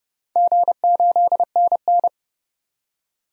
Morse code, heard as a keyed tone, about 32 wpm, 700 Hz. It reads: G8NN
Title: DXCC-100 Call Signs - 4 Characters - 30wpm